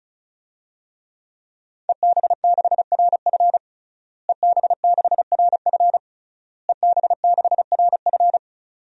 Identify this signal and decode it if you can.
Morse code, heard as a keyed tone, about 34 wpm, 700 Hz. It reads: EB6RF EB6RF EB6RF
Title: DXCC-100 Call Signs - 5 Characters - Repeated 3x - 35wpm